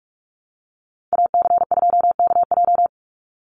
Morse code, heard as a keyed tone, about 44 wpm, 700 Hz. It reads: AC2KJ
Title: Call Signs Repeated 2x - 45wpm